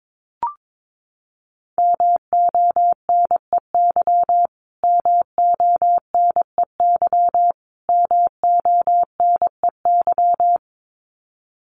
Morse code, heard as a keyed tone, about 22 wpm, 700 Hz. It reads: MONEY MONEY MONEY
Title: Top 500 Words Repeated 22wpm